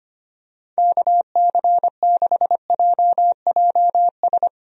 Morse code, heard as a keyed tone, about 25 wpm, 700 Hz. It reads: KC6JJS
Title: USA Call Signs - 6 Characters - Repeated 2x - 25wpm